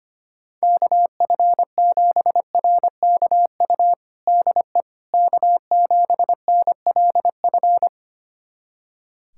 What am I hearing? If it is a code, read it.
Morse code, heard as a keyed tone, about 25 wpm, 700 Hz. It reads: KF7RKU DE K7NLF
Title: USA Callsign DE Callsign 25wpm